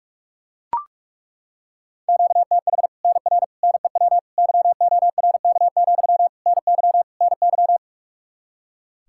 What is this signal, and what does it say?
Morse code, heard as a keyed tone, about 42 wpm, 700 Hz. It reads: QTH NR NEW YORK, NY NY
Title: Practice QSO - Exchange 1 - Beginning - 45wpm